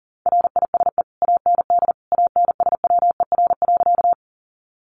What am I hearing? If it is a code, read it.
Morse code, heard as a keyed tone, about 40 wpm, 700 Hz. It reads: RISE AND ANSWER.
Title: Sentences from Top 900 Words 40wpm